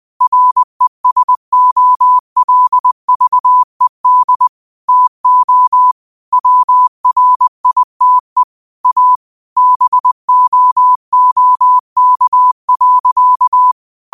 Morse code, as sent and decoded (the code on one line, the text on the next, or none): .-. ... --- .-.. ...- . -.. / - --- / .-- .-. .. - . / .- / -... --- --- -.- .-.-.-
RESOLVED TO WRITE A BOOK.